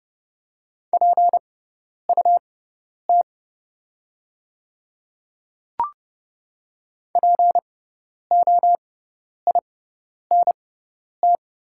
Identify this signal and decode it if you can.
Morse code, heard as a keyed tone, about 30 wpm, spaced as for 10 wpm, 690 Hz. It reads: PUT POINT
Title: 6x Character Spacing - Top 500 Words - 30wpm